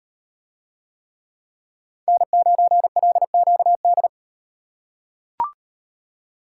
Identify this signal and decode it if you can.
Morse code, heard as a keyed tone, about 38 wpm, 700 Hz. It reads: N9LQD